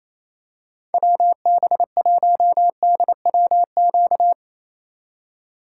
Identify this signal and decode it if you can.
Morse code, heard as a keyed tone, about 28 wpm, 690 Hz. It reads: WB1DWQ
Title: USA Call Signs - 6 Characters - 2x Character Spacing - 28wpm